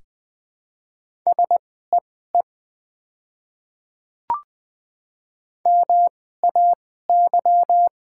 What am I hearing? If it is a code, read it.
Morse code, heard as a keyed tone, about 20 wpm, spaced as for 14 wpm, 690 Hz. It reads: SEE MAY
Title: 2x Character Spacing - Top 100 Words - 20wpm